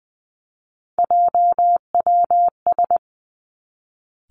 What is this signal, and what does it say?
Morse code, heard as a keyed tone, about 20 wpm, 700 Hz. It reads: JWS
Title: Three Challenging Letter-Number Combinations 20wpm